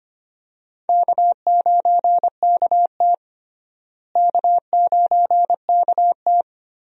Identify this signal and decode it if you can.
Morse code, heard as a keyed tone, about 25 wpm, 700 Hz. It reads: K9KT K9KT
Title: USA Call Signs - 4 Characters - Repeated 2x - 25wpm